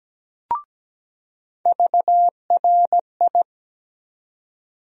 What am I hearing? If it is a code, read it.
Morse code, heard as a keyed tone, about 17 wpm, 700 Hz. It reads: VRI
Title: USA Call Sign Postfixes - 3 Letters 17wpm